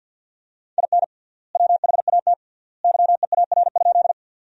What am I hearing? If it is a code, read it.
Morse code, heard as a keyed tone, about 52 wpm, 700 Hz. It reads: IN WHAT YEAR?
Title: 3x Word-Spacing - Sentences from Top 200 Words - 50wpm